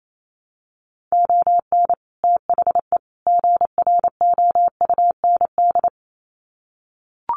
Morse code, heard as a keyed tone, about 28 wpm, 700 Hz. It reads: ON THE GROUND
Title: Sets of 3 Words 28wpm